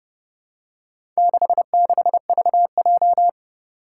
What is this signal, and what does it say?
Morse code, heard as a keyed tone, about 30 wpm, 700 Hz. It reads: 66VJ
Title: Four Challenging Letter-Number Combinations 30wpm